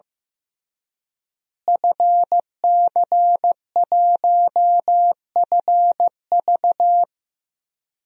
Morse code, heard as a keyed tone, about 16 wpm, 700 Hz. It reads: FC1FV
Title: DXCC-100 Call Signs - 5 Characters - Repeated 3x - 15wpm